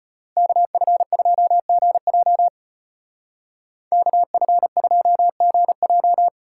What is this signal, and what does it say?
Morse code, heard as a keyed tone, about 30 wpm, 700 Hz. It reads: KF2GJ KF2GJ
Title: Speed Racing - USA Call Signs 25wpm